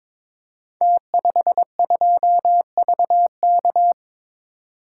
Morse code, heard as a keyed tone, about 23 wpm, 700 Hz. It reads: T52VK
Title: Code Groups - 5 Letter:Numbers - 22wpm